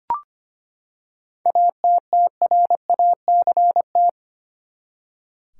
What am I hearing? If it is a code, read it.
Morse code, heard as a keyed tone, about 25 wpm, 700 Hz. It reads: ATTRACT
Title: Seven Letter Words - 25wpm